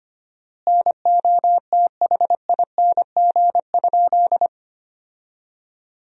Morse code, heard as a keyed tone, about 25 wpm, 700 Hz. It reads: NOTHING?
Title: 4x Word-Spacing - Sentences from Top 500 Words - 25wpm